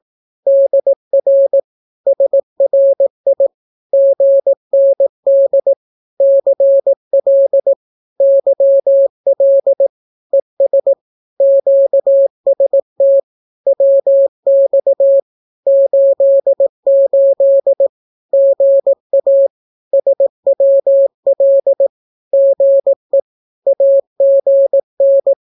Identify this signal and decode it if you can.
Morse code, heard as a keyed tone, about 18 wpm, 550 Hz. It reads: DR SRI GND CL YL ES QST WX 88 GA SWL GE AGN
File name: naukapodstaw / skroty / skroty